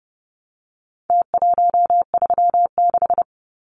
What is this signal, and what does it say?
Morse code, heard as a keyed tone, about 30 wpm, 700 Hz. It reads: T136
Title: Numbers - Contest Style - 4-Digit 30wpm